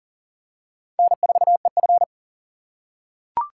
Morse code, heard as a keyed tone, about 40 wpm, 700 Hz. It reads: N4EF
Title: USA Call Signs - 4 Characters - 2x Character Spacing - 40wpm